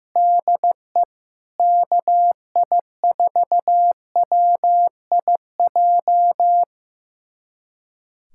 Morse code, heard as a keyed tone, about 15 wpm, 700 Hz. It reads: DE KI4WIJ
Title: USA Callsign DE Callsign - 6 Characters - 15wpm